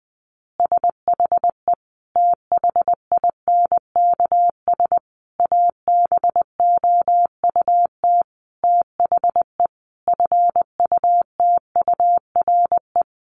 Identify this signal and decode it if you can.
Morse code, heard as a keyed tone, about 20 wpm, 700 Hz. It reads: SHE THINKS ABOUT THE FUTURE
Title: Easy Phrases 20wpm